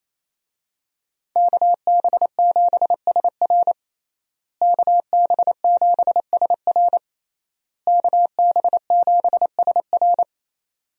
Morse code, heard as a keyed tone, about 28 wpm, 700 Hz. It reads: KB7SR KB7SR KB7SR
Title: USA Call Signs - 5 Characters - Repeated 3x - 28wpm